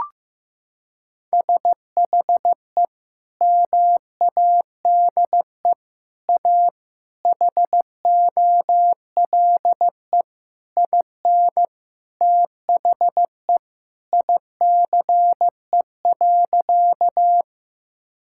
Morse code, heard as a keyed tone, about 16 wpm, 700 Hz. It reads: SHE MADE A HOLE IN THE ICE.
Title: Sentences from Top 1600 Words 15wpm